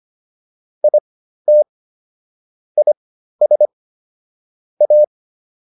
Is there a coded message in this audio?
Yes: Morse code from 0.8 to 5.0 s, heard as a keyed tone, about 25 wpm, spaced as for 13 wpm, 600 Hz. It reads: IT IS A